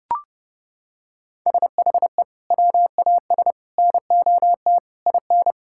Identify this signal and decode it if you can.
Morse code, heard as a keyed tone, about 32 wpm, 700 Hz. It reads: SHE WAS NOT IN
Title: Sentences from Top 1000 Words - Review Part 5 - 30wpm